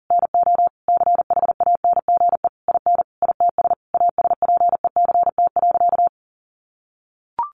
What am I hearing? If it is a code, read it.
Morse code, heard as a keyed tone, about 40 wpm, 700 Hz. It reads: NO CHANGE IN ITS ASPECT.
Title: Sentences from Top 1300 Words 40wpm